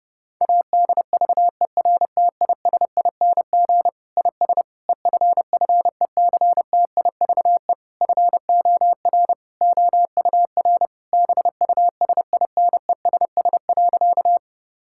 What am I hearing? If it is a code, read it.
Morse code, heard as a keyed tone, about 30 wpm, 700 Hz. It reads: ADVERTISING IS EFFECTIVE FOR OUR BUSINESS.
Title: Sentences from Top 1000 Words - Review Part 6 - 30wpm